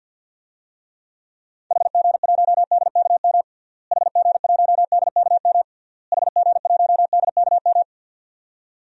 Morse code, heard as a keyed tone, about 52 wpm, 700 Hz. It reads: HG1DKM HG1DKM HG1DKM
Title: DXCC-100 Call Signs - 6 Characters - Repeated 3x - 50wpm